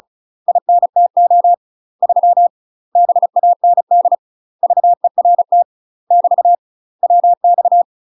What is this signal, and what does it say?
Morse code, heard as a keyed tone, about 34 wpm, 700 Hz. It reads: INTO 3 BAND VERT <BT> WX
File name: UK MASTER QSOs TRAINING / 35WPM / UK MASTER QSOs 0039